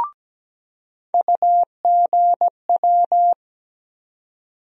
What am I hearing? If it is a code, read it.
Morse code, heard as a keyed tone, about 17 wpm, 700 Hz. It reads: UGW
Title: USA Call Sign Postfixes - 3 Letters 17wpm